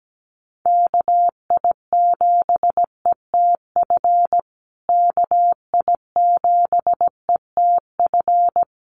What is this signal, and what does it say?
Morse code, heard as a keyed tone, about 17 wpm, 700 Hz. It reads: KI7ETF KI7ETF
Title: USA Call Signs Repeated 3x 17wpm